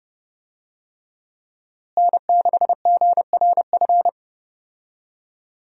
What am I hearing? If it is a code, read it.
Morse code, heard as a keyed tone, about 30 wpm, 700 Hz. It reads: N6GRF
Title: USA Call Signs - 5 Characters - Repeated 3x - 30wpm